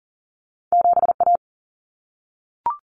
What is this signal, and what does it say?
Morse code, heard as a keyed tone, about 40 wpm, 700 Hz. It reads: 7 A
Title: Common Cut Numbers - 2-Digit 40wpm